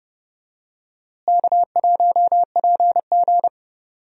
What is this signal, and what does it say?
Morse code, heard as a keyed tone, about 30 wpm, 700 Hz. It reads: K1PG
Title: USA Call Signs - 4 Characters - Repeated 3x - 30wpm